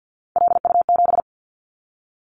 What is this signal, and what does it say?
Morse code, heard as a keyed tone, about 50 wpm, 700 Hz. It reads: LU7
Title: Callsign Prefixes for Top 100 DXCC - 50wpm